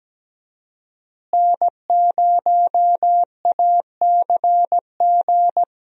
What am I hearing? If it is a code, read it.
Morse code, heard as a keyed tone, about 17 wpm, 700 Hz. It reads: N0ACG
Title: USA Call Signs - 5 Characters - Repeated 3x - 17wpm